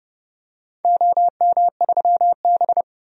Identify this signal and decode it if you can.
Morse code, heard as a keyed tone, about 32 wpm, 700 Hz. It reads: OM3B
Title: DXCC-100 Call Signs - 4 Characters - 30wpm